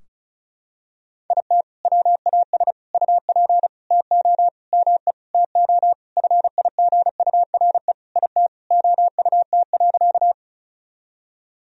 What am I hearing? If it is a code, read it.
Morse code, heard as a keyed tone, about 36 wpm, 700 Hz. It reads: IT WAS UP TO ME TO FIGURE IT OUT.
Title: Sentences from Top 1000 Words - Review Part 2 - 35wpm